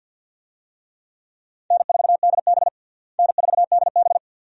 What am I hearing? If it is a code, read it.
Morse code, heard as a keyed tone, about 47 wpm, 700 Hz. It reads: N4DB N4DB
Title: USA Call Signs - 4 Characters - Repeated 2x - 50wpm